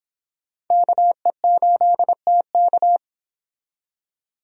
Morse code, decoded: KE8TK

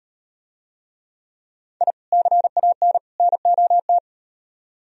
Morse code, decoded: I CAN NOT

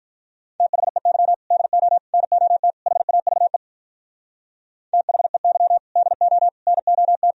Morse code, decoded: THEY DO NOT HAVE THEY DO NOT

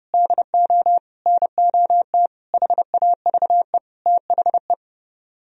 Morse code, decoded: DO NOT HAVE THE

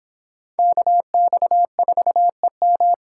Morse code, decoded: KX4EM

